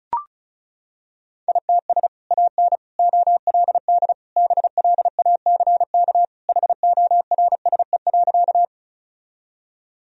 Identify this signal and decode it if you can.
Morse code, heard as a keyed tone, about 36 wpm, 700 Hz. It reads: ITS AN OLD BLACK HORSE.
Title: Sentences from Top 1000 Words - Review Part 2 - 35wpm